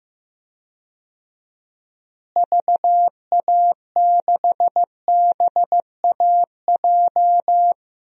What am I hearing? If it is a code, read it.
Morse code, heard as a keyed tone, about 16 wpm, 700 Hz. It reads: VA6BAJ
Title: DXCC-100 Call Signs - 6 Characters - Repeated 3x - 15wpm